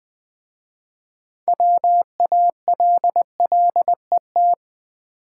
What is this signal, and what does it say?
Morse code, heard as a keyed tone, about 20 wpm, 700 Hz. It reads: WALLET
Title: Easy Words with Hints - 20wpm